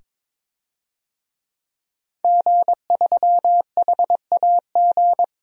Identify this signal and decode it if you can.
Morse code, heard as a keyed tone, about 23 wpm, 700 Hz. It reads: G3HAG
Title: DXCC-100 Call Signs - 5 Characters - 22wpm